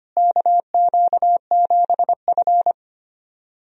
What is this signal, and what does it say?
Morse code, heard as a keyed tone, about 25 wpm, 700 Hz. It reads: KQ7F
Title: USA Call Signs - 4 Characters - Repeated 2x - 25wpm